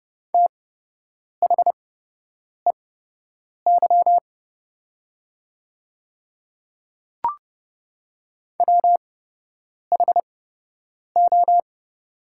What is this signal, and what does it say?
Morse code, heard as a keyed tone, about 30 wpm, spaced as for 8 wpm, 700 Hz. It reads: THEY WHO
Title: 8x Character Spacing - Top 100 Words - 30wpm